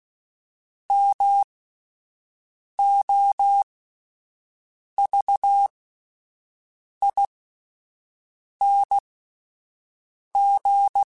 Morse code, heard as a keyed tone, about 16 wpm, spaced as for 5 wpm, 800 Hz. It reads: MOVING
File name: (spaced) / QSOs Spaced-mp3s / WD0BGP 1 @10 sp